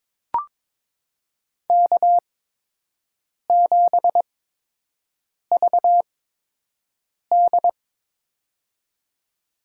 Morse code, heard as a keyed tone, about 22 wpm, spaced as for 6 wpm, 700 Hz. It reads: K7VD
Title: USA Call Signs - 4 Characters - 8x Character Spacing - 22wpm